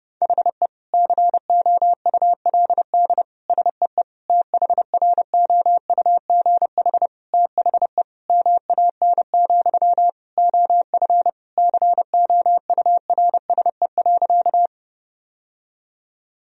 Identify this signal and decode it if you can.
Morse code, heard as a keyed tone, about 32 wpm, 700 Hz. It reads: HE COULD SEE THROUGH THE MAN, OF COURSE.
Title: Sentences from Top 1000 Words - Review Part 1 - 30wpm